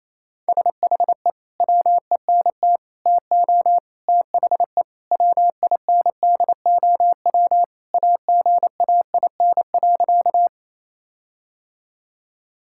Morse code, heard as a keyed tone, about 28 wpm, 700 Hz. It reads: SHE WENT TO THE WINDOW AGAIN.
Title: Sentences from Top 1600 Words 28wpm